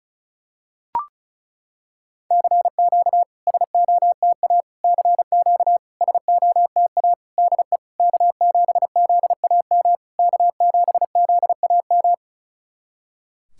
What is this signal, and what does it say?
Morse code, heard as a keyed tone, about 36 wpm, 700 Hz. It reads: CQ SOTA CQ SOTA DE K7ZAM K7ZAM
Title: SOTA QSO Practice 35wpm